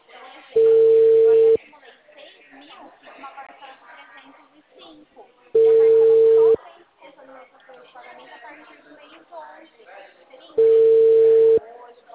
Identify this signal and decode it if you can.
Ringback tone (China, 450 Hz), with speech.